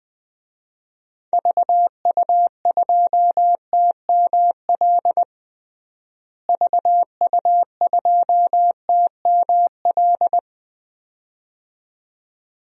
Morse code, heard as a keyed tone, about 20 wpm, 700 Hz. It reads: VU2TML VU2TML
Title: DXCC-100 Call Signs - 6 Characters - Repeated 2x - 20wpm